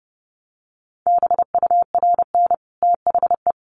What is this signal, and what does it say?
Morse code, heard as a keyed tone, about 30 wpm, 700 Hz. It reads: BURN THE
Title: Sentences from Top 1300 Words 30wpm